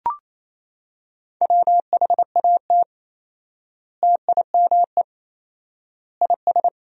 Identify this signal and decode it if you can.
Morse code, heard as a keyed tone, about 28 wpm, 700 Hz. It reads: WHAT TIME IS
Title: 4x Word-Spacing - Sentences from Top 100 Words - 28wpm